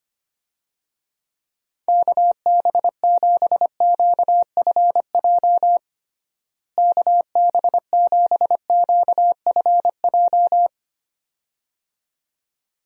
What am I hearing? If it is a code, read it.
Morse code, heard as a keyed tone, about 25 wpm, 700 Hz. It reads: KB7QFJ KB7QFJ
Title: USA Call Signs - 6 Characters - Repeated 2x - 25wpm